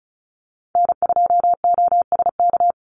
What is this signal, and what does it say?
Morse code, heard as a keyed tone, about 34 wpm, 700 Hz. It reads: N2OSK
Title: USA Call Signs 35wpm